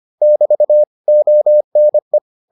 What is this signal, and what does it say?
Morse code, heard as a keyed tone, about 25 wpm, 600 Hz. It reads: <BT> ONE